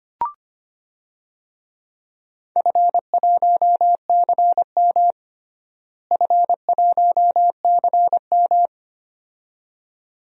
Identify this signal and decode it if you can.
Morse code, heard as a keyed tone, about 25 wpm, 700 Hz. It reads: F1CM F1CM
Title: DXCC-100 Call Signs - 4 Characters - Repeated 2x - 25wpm